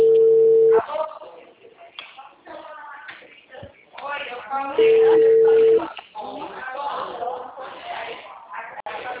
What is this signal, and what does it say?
Ringback tone (China, 450 Hz), with speech.